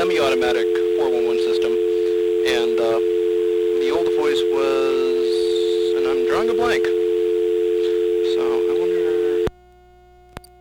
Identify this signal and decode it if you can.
Dial tone (North America), 350 + 440 Hz, with speech.
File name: blackratchet / recordings / Verizon411